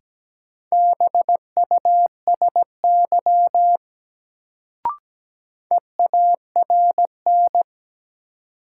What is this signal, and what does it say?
Morse code, heard as a keyed tone, about 17 wpm, 700 Hz. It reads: BUSY EARN
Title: Sentences from Top 1500 Words - New Word Drill Alt - 17wpm